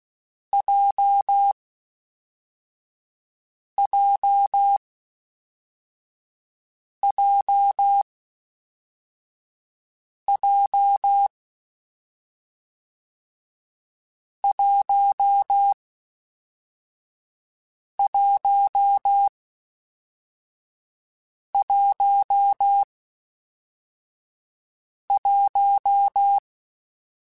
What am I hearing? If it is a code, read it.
Morse code, heard as a keyed tone, about 16 wpm, 800 Hz. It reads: J J J J 1 1 1 1
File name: Difficult Combos / Difficult Combos-mp3s / WPJ WPJ1 @10